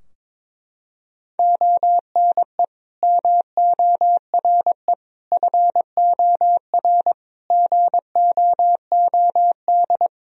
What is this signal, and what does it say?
Morse code, heard as a keyed tone, about 22 wpm, 700 Hz. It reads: ONE MORE FOR GOOD
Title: Sentences from Top 1000 Words - Review Part 2 - 22wpm